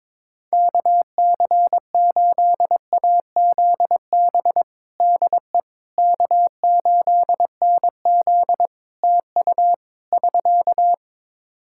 Morse code, heard as a keyed tone, about 22 wpm, 700 Hz. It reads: KC8AZB DE K8NZ TU <SK>
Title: Practice QSO - Exchange 4 - Ending - 22wpm